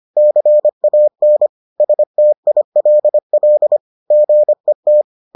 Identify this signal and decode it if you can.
Morse code, heard as a keyed tone, about 25 wpm, 600 Hz. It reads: CAN STILL GET